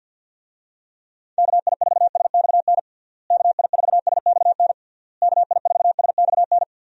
Morse code, heard as a keyed tone, about 49 wpm, 700 Hz. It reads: KI4SXN KI4SXN KI4SXN
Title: USA Call Signs - 6 Characters - Repeated 3x - 50wpm